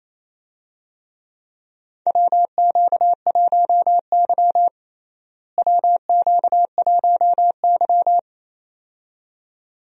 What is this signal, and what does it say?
Morse code, heard as a keyed tone, about 28 wpm, 700 Hz. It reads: WQ1Y WQ1Y
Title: USA Call Signs - 4 Characters - Repeated 2x - 28wpm